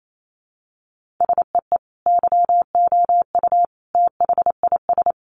Morse code, heard as a keyed tone, about 28 wpm, 700 Hz. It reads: SEE YOU THIS
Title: Sentences from Top 1500 Words 28wpm